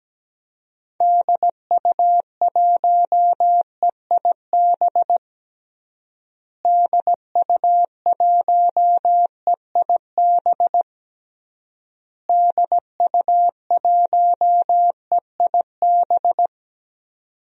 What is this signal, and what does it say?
Morse code, heard as a keyed tone, about 17 wpm, 700 Hz. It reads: DU1EIB DU1EIB DU1EIB
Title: DXCC-100 Call Signs - 6 Characters - Repeated 3x - 17wpm